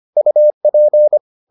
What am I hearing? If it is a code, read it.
Morse code, heard as a keyed tone, about 25 wpm, 600 Hz. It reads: UP